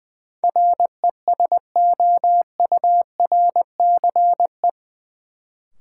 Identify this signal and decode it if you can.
Morse code, heard as a keyed tone, about 20 wpm, 700 Hz. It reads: RESOURCE
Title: Sentences from Top 1300 Words - New Word Drill Alt - 20wpm